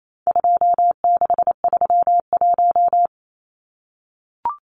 Morse code, heard as a keyed tone, about 28 wpm, 700 Hz. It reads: 2631
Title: Numbers - 4-Digit 28wpm